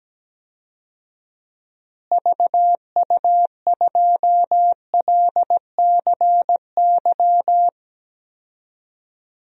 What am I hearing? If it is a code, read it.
Morse code, heard as a keyed tone, about 17 wpm, 700 Hz. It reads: VU2LCY
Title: DXCC-100 Call Signs - 6 Characters - Repeated 2x - 17wpm